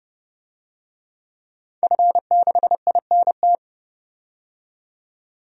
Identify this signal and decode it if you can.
Morse code, heard as a keyed tone, about 32 wpm, 700 Hz. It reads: F6INT
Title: DXCC-100 Call Signs - 5 Characters - 30wpm